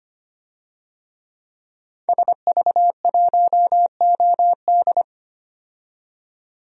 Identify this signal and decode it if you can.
Morse code, heard as a keyed tone, about 25 wpm, 700 Hz. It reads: SV1OD
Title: DXCC-100 Call Signs - 5 Characters - 25wpm